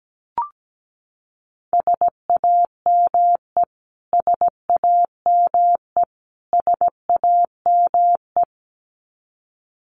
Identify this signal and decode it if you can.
Morse code, heard as a keyed tone, about 17 wpm, 700 Hz. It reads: SAME SAME SAME
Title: Top 500 Words Repeated 17wpm